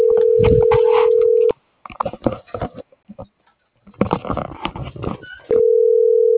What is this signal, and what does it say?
Ringback tone (North America), 440 + 480 Hz.